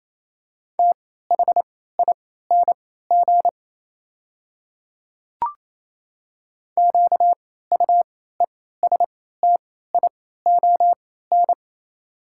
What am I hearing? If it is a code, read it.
Morse code, heard as a keyed tone, about 27 wpm, spaced as for 16 wpm, 700 Hz. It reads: THING QUESTION
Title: 3x Character Spacing - Top 500 Words - 28wpm